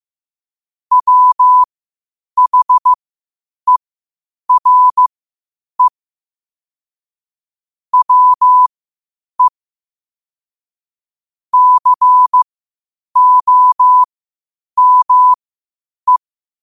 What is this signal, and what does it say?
Morse code, heard as a keyed tone, about 15 wpm, spaced as for 8 wpm, 1000 Hz. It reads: WHERE WE COME
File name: qotdfiles / 8wpm